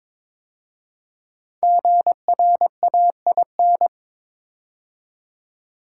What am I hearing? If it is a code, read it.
Morse code, heard as a keyed tone, about 22 wpm, 700 Hz. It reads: GRAIN
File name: 5 Letter Words - First 3 Letters Given - 22wpm